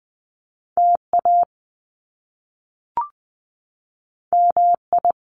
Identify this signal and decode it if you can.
Morse code, heard as a keyed tone, about 20 wpm, 700 Hz. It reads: TA MI
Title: Top Two Letter Combinations in Words 20wpm